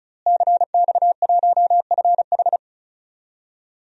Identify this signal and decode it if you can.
Morse code, heard as a keyed tone, about 36 wpm, 700 Hz. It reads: CX1FH
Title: DXCC-100 Call Signs - 5 Characters - Repeated 2x - 35wpm